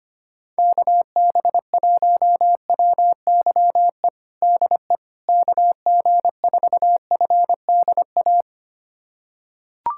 Morse code, heard as a keyed tone, about 25 wpm, 700 Hz. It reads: KB1WYE DE KG4FDA
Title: USA Callsign DE Callsign - 6 Characters - 25wpm